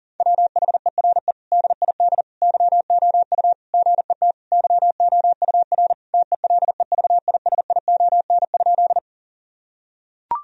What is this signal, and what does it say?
Morse code, heard as a keyed tone, about 40 wpm, 700 Hz. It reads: WHERE DID YOU GET YOUR TELEVISION?
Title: Sentences from Top 1000 Words - Review Part 4 - 40wpm